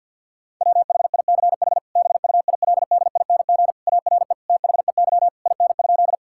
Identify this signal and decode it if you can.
Morse code, heard as a keyed tone, about 50 wpm, 700 Hz. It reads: WHICH BUILDING ARE THEY IN?
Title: Sentences from Top 1000 Words - Review Part 5 - 50wpm